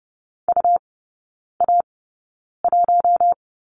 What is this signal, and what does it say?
Morse code, heard as a keyed tone, about 30 wpm, spaced as for 9 wpm, 700 Hz. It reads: UA1
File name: Lesson 18e - Letter m - Three Characters 30wpm